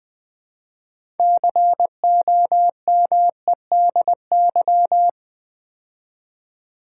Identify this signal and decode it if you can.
Morse code, heard as a keyed tone, about 20 wpm, 700 Hz. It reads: COMEDY
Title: Build A Word - 2 Letters At A Time - 6-Letter Words - 20wpm